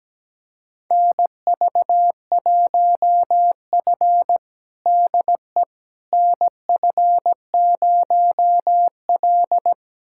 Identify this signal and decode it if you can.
Morse code, heard as a keyed tone, about 17 wpm, 700 Hz. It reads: NV1F DE NF0L
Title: USA Callsign DE Callsign - 4 Characters - 17wpm